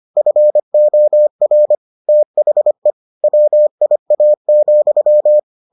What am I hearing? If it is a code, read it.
Morse code, heard as a keyed tone, about 25 wpm, 600 Hz. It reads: FOR THE WIA,